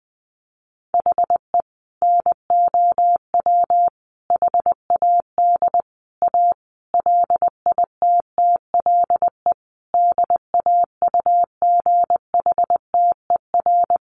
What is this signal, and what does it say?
Morse code, heard as a keyed tone, about 20 wpm, 700 Hz. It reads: HE NOW HAD A LITTLE DAUGHTER
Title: Sentences from Top 1500 Words 20wpm